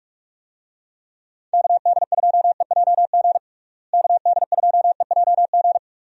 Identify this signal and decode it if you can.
Morse code, heard as a keyed tone, about 43 wpm, 700 Hz. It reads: KD2EJG KD2EJG
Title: USA Call Signs - 6 Characters - Repeated 3x - 45wpm